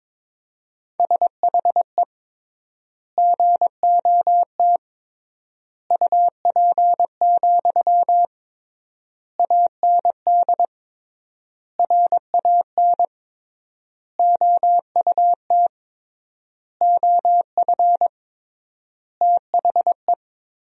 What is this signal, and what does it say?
Morse code, heard as a keyed tone, about 22 wpm, 700 Hz. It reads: SHE GOT UP, AND RAN OUT OF THE
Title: 3x Word-Spacing - Sentences from Top 500 Words - 22wpm